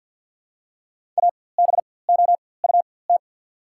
Morse code, heard as a keyed tone, about 54 wpm, spaced as for 24 wpm, 700 Hz. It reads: ABOUT
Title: 4x Character Spacing - Top 300 Words - 50wpm